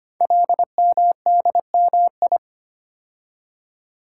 Morse code, heard as a keyed tone, about 25 wpm, 700 Hz. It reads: LMDMI
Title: Code Groups - 5 Letters - 25wpm